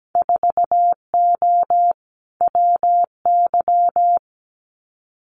Morse code, heard as a keyed tone, about 17 wpm, 700 Hz. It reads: 4O WY
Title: Winter Field Day Exchanges Encore 17wpm